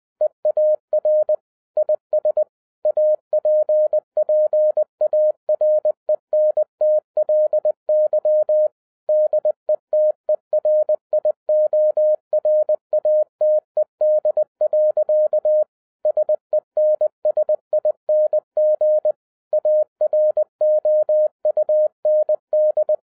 Morse code, heard as a keyed tone, about 20 wpm, 610 Hz. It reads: EAR IS APPARENTLY DETERIORATED. SENSING AROUND
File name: 2011-07-mp3 / n58 BACK FROM A BIG ACCIDENT 20wpm